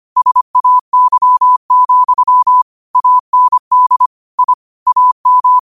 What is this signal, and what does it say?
Morse code, heard as a keyed tone, about 25 wpm, 1000 Hz. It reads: SAY, AND I AM
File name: 25wpm